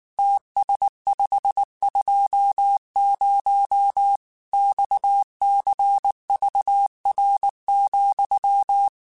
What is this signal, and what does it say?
Morse code, heard as a keyed tone, about 19 wpm, 800 Hz. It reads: TS520 XCVR,
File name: QSOs / QSOs-mp3s / WD0FBT 1@18